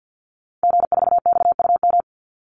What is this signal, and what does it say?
Morse code, heard as a keyed tone, about 48 wpm, 700 Hz. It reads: G4XUM
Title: Call Signs - 50wpm